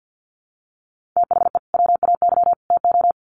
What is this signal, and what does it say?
Morse code, heard as a keyed tone, about 50 wpm, 700 Hz. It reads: THE WAY TO
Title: Sets of 3 Words 50wpm